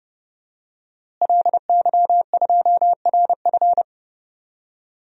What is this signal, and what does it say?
Morse code, heard as a keyed tone, about 30 wpm, 700 Hz. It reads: LY2RF
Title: DXCC-100 Call Signs - 5 Characters - Repeated 2x - 30wpm